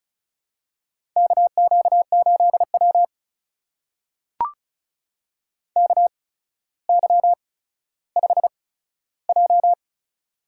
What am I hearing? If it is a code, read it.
Morse code, heard as a keyed tone, about 36 wpm, spaced as for 15 wpm, 700 Hz. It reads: KQ8W KY5J
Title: USA Call Signs - 4 Characters - 8x Character Spacing - 35wpm